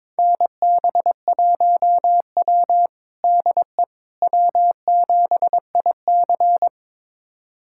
Morse code, heard as a keyed tone, about 22 wpm, 700 Hz. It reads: NB1W DE W7IC
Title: USA Callsign DE Callsign - 4 Characters - 22wpm